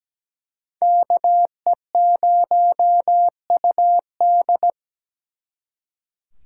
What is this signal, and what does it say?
Morse code, heard as a keyed tone, about 20 wpm, 700 Hz. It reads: KE0UD
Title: Speed Racing - USA 2x2 Call Signs 17wpm